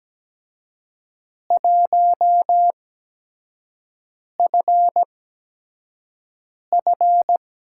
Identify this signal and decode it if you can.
Morse code, heard as a keyed tone, about 17 wpm, spaced as for 5 wpm, 700 Hz. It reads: 1FF
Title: USA Call Signs - 4 Characters - 8x Character Spacing - 17wpm